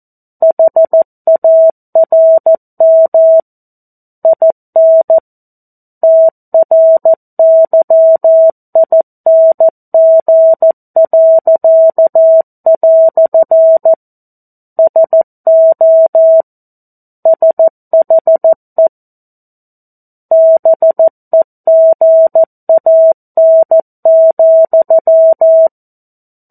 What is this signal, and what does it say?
Morse code, heard as a keyed tone, about 14 wpm, 650 Hz. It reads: HARM IN TRYING." SO SHE BEGAN,
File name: 02-Chapter-02 14 WPM